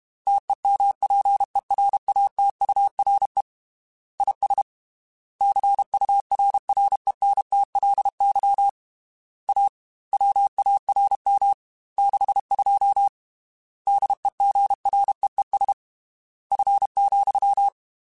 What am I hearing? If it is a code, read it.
Morse code, heard as a keyed tone, about 32 wpm, 800 Hz. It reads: TEMPERATURE IS CURRENTLY A WARM 62 DEGREES F,
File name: Weather Sets / Weather Sets-mp3s / Weather 4 28 @30